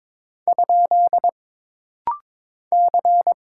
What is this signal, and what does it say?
Morse code, heard as a keyed tone, about 22 wpm, 700 Hz. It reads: ? C
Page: This rapid-fire timing is geared to push your ICR (Instant Character Recognition) skills.